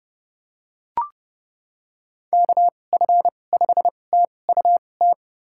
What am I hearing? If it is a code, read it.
Morse code, heard as a keyed tone, about 31 wpm, spaced as for 22 wpm, 700 Hz. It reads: KF5TUT
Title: USA Call Signs - 6 Characters - 2x Character Spacing - 30wpm